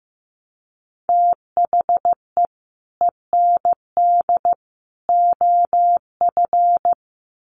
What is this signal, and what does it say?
Morse code, heard as a keyed tone, about 15 wpm, 700 Hz. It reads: THE END OF
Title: Sets of 3 Words 15wpm